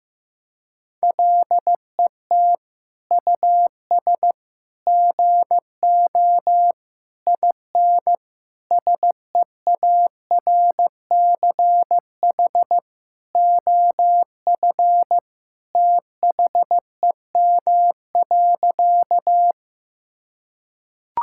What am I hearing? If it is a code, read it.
Morse code, heard as a keyed tone, about 15 wpm, 700 Hz. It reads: LET US GO IN SEARCH OF THEM.
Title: Sentences from Top 1000 Words - Review Part 5 - 15wpm